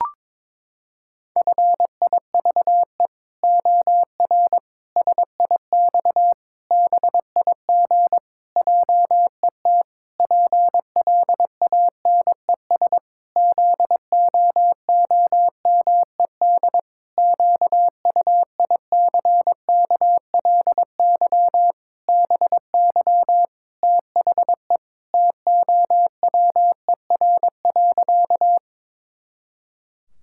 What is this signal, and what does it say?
Morse code, heard as a keyed tone, about 22 wpm, 700 Hz. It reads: FIVE OR SIX BIG JET PLANES ZOOMED QUICKLY BY THE TOWER.
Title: Pangram Challenge v2 - 22wpm